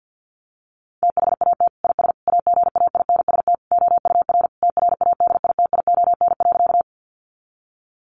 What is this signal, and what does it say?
Morse code, heard as a keyed tone, about 50 wpm, 700 Hz. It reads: THAT IS AGAINST OUR TRADITION.
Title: Sentences from Top 1500 Words 50wpm